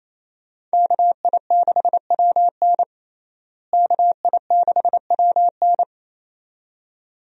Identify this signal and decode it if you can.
Morse code, heard as a keyed tone, about 28 wpm, 700 Hz. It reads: KI6WN KI6WN
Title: USA Call Signs - 5 Characters - Repeated 2x - 28wpm